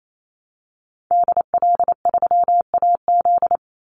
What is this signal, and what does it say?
Morse code, heard as a keyed tone, about 28 wpm, 700 Hz. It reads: DL3AZ
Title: Call Signs - 28wpm